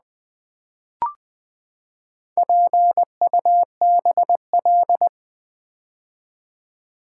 Morse code, heard as a keyed tone, about 20 wpm, 700 Hz. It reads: PUBL
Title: Build A Word - 4 Letters At A Time - 10-Letter Words - 20wpm